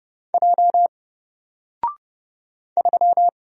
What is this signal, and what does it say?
Morse code, heard as a keyed tone, about 31 wpm, 700 Hz. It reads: J 3
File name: Single-Letter-Number-30wpm